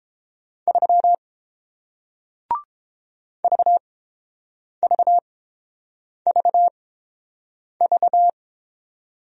Morse code, heard as a keyed tone, about 31 wpm, spaced as for 7 wpm, 700 Hz. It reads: 3 VVVV